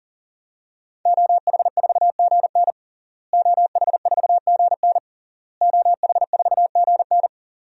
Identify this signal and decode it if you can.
Morse code, heard as a keyed tone, about 40 wpm, 700 Hz. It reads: OH4GN OH4GN OH4GN
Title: DXCC-100 Call Signs - 5 Characters - Repeated 3x - 40wpm